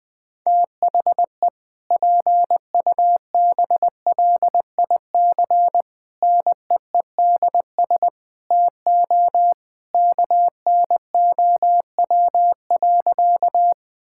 Morse code, heard as a keyed tone, about 20 wpm, 700 Hz. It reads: THE PUBLIC NEEDS TO KNOW.
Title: Sentences with 2-Word Phrases - 20wpm